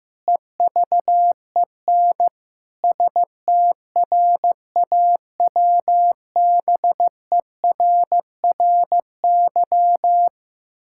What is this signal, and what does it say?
Morse code, heard as a keyed tone, about 15 wpm, 700 Hz. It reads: EVEN STRAWBERRY
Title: Pangram Challenge v2 - 15wpm